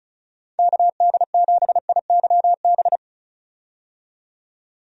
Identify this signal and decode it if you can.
Morse code, heard as a keyed tone, about 34 wpm, 700 Hz. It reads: KD7IYB
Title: USA Call Signs - 6 Characters - 35wpm